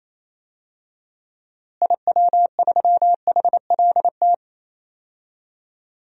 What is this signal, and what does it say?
Morse code, heard as a keyed tone, about 28 wpm, 700 Hz. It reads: IW3HLT
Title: DXCC-100 Call Signs - 6 Characters - Repeated 3x - 28wpm